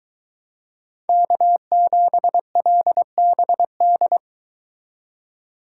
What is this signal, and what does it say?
Morse code, heard as a keyed tone, about 22 wpm, 700 Hz. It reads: K7LBD